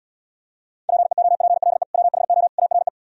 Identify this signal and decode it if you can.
Morse code, heard as a keyed tone, about 76 wpm, 700 Hz. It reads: PEOPLE WHO ARE